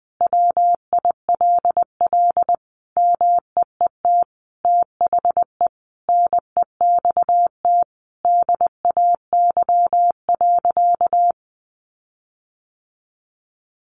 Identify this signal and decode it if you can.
Morse code, heard as a keyed tone, about 20 wpm, 700 Hz. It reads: WILL MEET THE NEXT DAY.
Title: Sentences from Top 1400 Words 20wpm